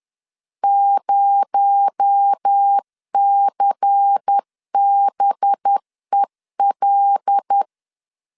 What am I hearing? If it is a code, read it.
Morse code, heard as a keyed tone, about 10 wpm, 800 Hz. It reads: 0CBEL
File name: CW / DET / 1 6 1500C S A 10wpm 4141